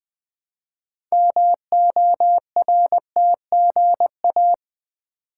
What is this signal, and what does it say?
Morse code, heard as a keyed tone, about 20 wpm, 700 Hz. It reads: MORTGA
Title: Build A Word - 3 Letters At A Time - 8-Letter Words - 20wpm